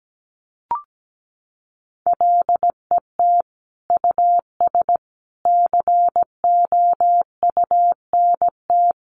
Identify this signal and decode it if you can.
Morse code, heard as a keyed tone, about 17 wpm, 700 Hz. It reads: LET US COUNT
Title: Sentences from Top 1500 Words 17wpm